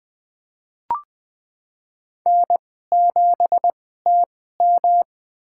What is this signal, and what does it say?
Morse code, heard as a keyed tone, about 20 wpm, spaced as for 15 wpm, 700 Hz. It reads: N7TM
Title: USA Call Signs - 2x Character Spacing - 20wpm